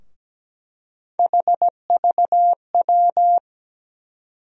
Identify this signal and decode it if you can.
Morse code, heard as a keyed tone, about 17 wpm, 700 Hz. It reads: HVW
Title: USA Call Sign Postfixes - 3 Letters 17wpm